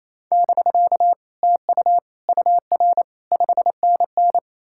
Morse code, heard as a keyed tone, about 38 wpm, 700 Hz. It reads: <BK> TU UR 5NN